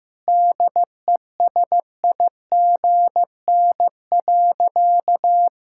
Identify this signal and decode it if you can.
Morse code, heard as a keyed tone, about 16 wpm, 700 Hz. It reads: DESIGN.
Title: Sentences from Top 1000 Words - Review Part 5 - 15wpm